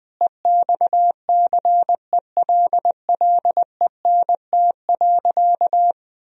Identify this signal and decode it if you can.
Morse code, heard as a keyed tone, about 20 wpm, 700 Hz. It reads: EXCELLENT.
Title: Sentences from Top 1000 Words - Review Part 4 - 20wpm